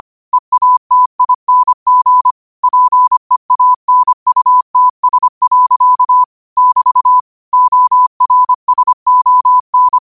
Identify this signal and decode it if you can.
Morse code, heard as a keyed tone, about 25 wpm, 1000 Hz. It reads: EATING PEANUTS. <BT> ORSON
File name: qotdfiles / 25wpm